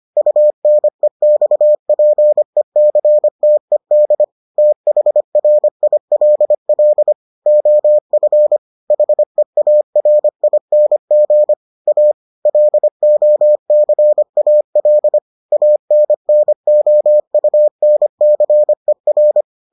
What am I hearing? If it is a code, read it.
Morse code, heard as a keyed tone, about 25 wpm, 600 Hz. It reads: UNEXPECTED THRILL OF HEARING A LOCAL ANNOUNCER